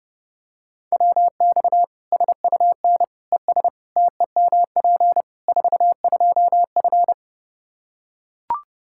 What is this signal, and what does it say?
Morse code, heard as a keyed tone, about 32 wpm, 700 Hz. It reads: WX SUN ES TEMP 42F
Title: K1USN SST QSO Practice 30wpm